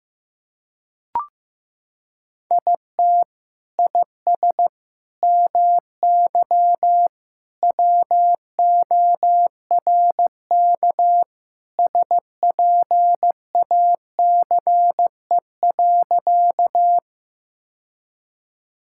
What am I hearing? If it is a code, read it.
Morse code, heard as a keyed tone, about 15 wpm, 700 Hz. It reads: IT IS MY WORK SPACE.
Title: Sentences from Top 1000 Words - Review Part 5 - 15wpm